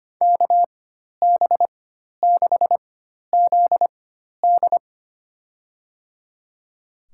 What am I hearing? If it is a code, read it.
Morse code, heard as a keyed tone, about 25 wpm, spaced as for 12 wpm, 700 Hz. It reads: KB6ZD
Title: USA Call Signs - 5 Characters - 4x Character Spacing - 25wpm